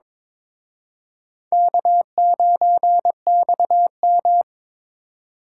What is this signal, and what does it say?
Morse code, heard as a keyed tone, about 26 wpm, 700 Hz. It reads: K9XM
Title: Speed Racing - USA 1x2 Call Signs 22wpm